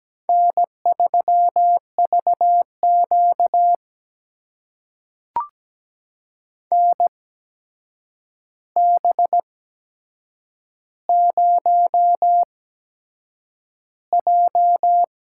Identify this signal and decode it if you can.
Morse code, heard as a keyed tone, about 17 wpm, spaced as for 7 wpm, 700 Hz. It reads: N3VQ NB0J
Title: USA Call Signs - 4 Characters - 8x Character Spacing - 17wpm